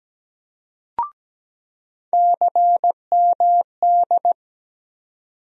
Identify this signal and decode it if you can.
Morse code, heard as a keyed tone, about 17 wpm, 700 Hz. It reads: CMD
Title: USA Call Sign Postfixes - 3 Letters 17wpm